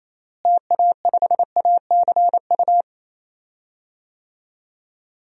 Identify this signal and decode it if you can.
Morse code, heard as a keyed tone, about 28 wpm, 700 Hz. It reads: TA5ACU
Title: DXCC-100 Call Signs - 6 Characters - Repeated 2x - 28wpm